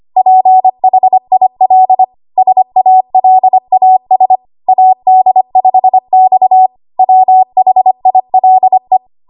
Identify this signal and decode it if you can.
Morse code, heard as a keyed tone, about 25 wpm, 750 Hz. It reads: PHIL SALAS AD5X WHILE
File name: Archive / 25 WPM / 25WPM